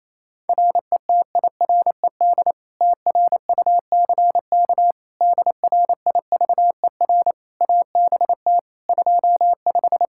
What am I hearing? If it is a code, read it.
Morse code, heard as a keyed tone, about 28 wpm, 700 Hz. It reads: RETIRED TRUCK DRIVER ABT 25
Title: Practice QSO - Exchange 3 - More Info - 28wpm